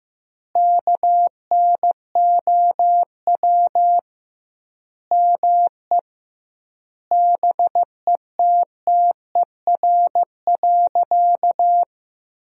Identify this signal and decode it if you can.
Morse code, heard as a keyed tone, about 15 wpm, 700 Hz. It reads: KNOW ME BETTER.
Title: 2x Word-Spacing - Sentences from Top 400 Words - 15wpm